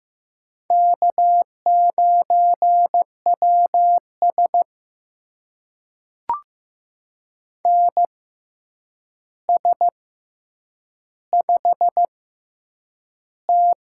Morse code, heard as a keyed tone, about 15 wpm, spaced as for 8 wpm, 700 Hz. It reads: K9WS NS5T
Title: USA Call Signs - 4 Characters - 6x Character Spacing - 15wpm